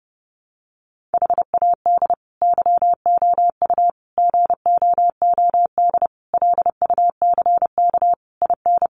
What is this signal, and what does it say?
Morse code, heard as a keyed tone, about 30 wpm, 700 Hz. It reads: HAD YOU GOOD LUCK IN
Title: Sentences from Top 1500 Words 30wpm